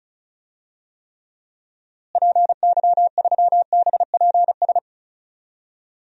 Morse code, heard as a keyed tone, about 36 wpm, 700 Hz. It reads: PY3BPS
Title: DXCC-100 Call Signs - 6 Characters - Repeated 2x - 35wpm